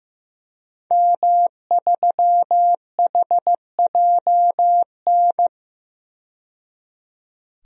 Morse code, heard as a keyed tone, about 16 wpm, 700 Hz. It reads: M3HJN
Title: DXCC-100 Call Signs - 5 Characters - 15wpm